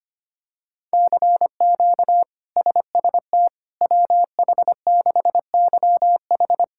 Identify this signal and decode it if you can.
Morse code, heard as a keyed tone, about 25 wpm, 700 Hz. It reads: CQ SST WH6YH
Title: K1USN SST QSO Practice - 25wpm